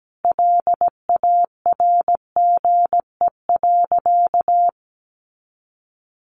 Morse code, heard as a keyed tone, about 17 wpm, 700 Hz. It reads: LARGE.
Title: Sentences from Top 700 Words 17wpm